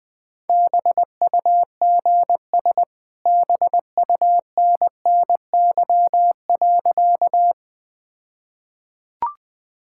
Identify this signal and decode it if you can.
Morse code, heard as a keyed tone, about 20 wpm, 700 Hz. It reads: BUGS BUNNY.
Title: All-Age Appropriate Jokes - 20wpm